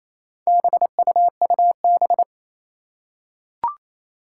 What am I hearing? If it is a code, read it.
Morse code, heard as a keyed tone, about 28 wpm, 700 Hz. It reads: BUUB
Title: Four Challenging Letter-Number Combinations 28wpm